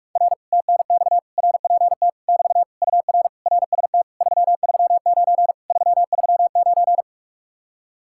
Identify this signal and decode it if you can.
Morse code, heard as a keyed tone, about 48 wpm, 700 Hz. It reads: R TNX RPT <BT> UR RST 339 339